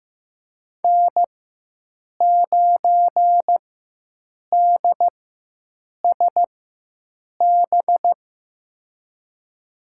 Morse code, heard as a keyed tone, about 15 wpm, spaced as for 7 wpm, 700 Hz. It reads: N9DSB
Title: USA Call Signs - 5 Characters - 4x Character Spacing - 15wpm